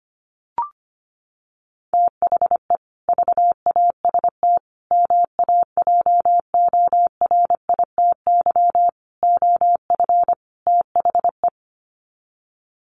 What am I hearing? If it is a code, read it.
Morse code, heard as a keyed tone, about 25 wpm, 700 Hz. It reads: THE VAST MAJORITY OF THE
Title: Sets of 5 Words Encore 25wpm